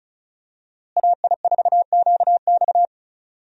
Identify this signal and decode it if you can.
Morse code, heard as a keyed tone, about 34 wpm, 700 Hz. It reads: AI4QX
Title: USA Call Signs - 5 Characters - Repeated 2x - 35wpm